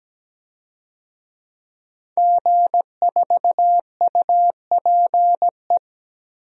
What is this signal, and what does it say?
Morse code, heard as a keyed tone, about 17 wpm, 700 Hz. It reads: G4UPE
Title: DXCC-100 Call Signs - 5 Characters - 17wpm